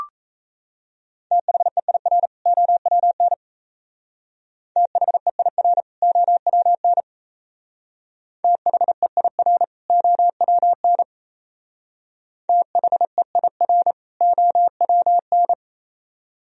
Morse code, heard as a keyed tone, about 34 wpm, 700 Hz. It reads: THEIR OWN THEIR OWN THEIR OWN THEIR OWN
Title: Speed-Racing - Sets of 2 Words 28wpm